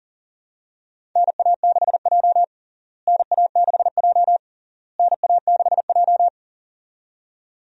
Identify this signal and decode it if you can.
Morse code, heard as a keyed tone, about 40 wpm, 700 Hz. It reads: NA6J NA6J NA6J
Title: USA Call Signs - 4 Characters - Repeated 3x - 40wpm